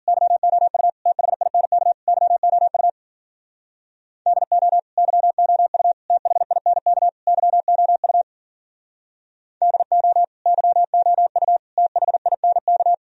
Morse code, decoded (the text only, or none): YOU THINK YOU DO YOU THINK YOU DO YOU THINK